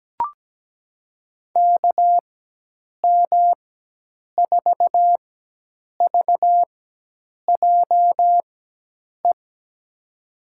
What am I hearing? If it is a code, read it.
Morse code, heard as a keyed tone, about 17 wpm, spaced as for 8 wpm, 700 Hz. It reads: KM4VJE
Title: USA Call Signs - 6 Characters - 4x Character Spacing - 17wpm